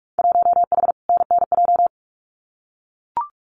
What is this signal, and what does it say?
Morse code, heard as a keyed tone, about 44 wpm, 700 Hz. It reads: 1H NNJ
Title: Winter Field Day Exchanges 45wpm